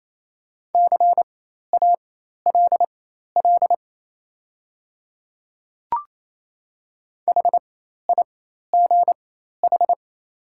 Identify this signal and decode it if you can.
Morse code, heard as a keyed tone, about 27 wpm, spaced as for 13 wpm, 690 Hz. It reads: CALL HIGH
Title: 4x Character Spacing - Top 300 Words - 28wpm